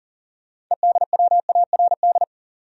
Morse code, heard as a keyed tone, about 40 wpm, 700 Hz. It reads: EDWARD
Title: USA Common Names - 40wpm